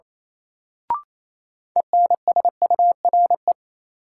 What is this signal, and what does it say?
Morse code, heard as a keyed tone, about 28 wpm, 700 Hz. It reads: ENSURE
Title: Sentences from Top 1100 Words - New Word Drill - 28wpm